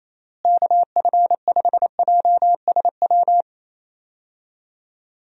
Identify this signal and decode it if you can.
Morse code, heard as a keyed tone, about 27 wpm, 700 Hz. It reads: KF5JSW
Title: USA Call Signs - 6 Characters - Repeated 3x - 28wpm